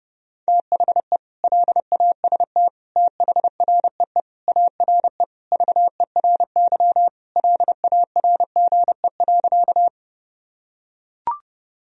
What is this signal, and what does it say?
Morse code, heard as a keyed tone, about 30 wpm, 700 Hz. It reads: THE LAST THREE ARE VERY LARGE.
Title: Sentences from Top 1000 Words - Review Part 1 - 30wpm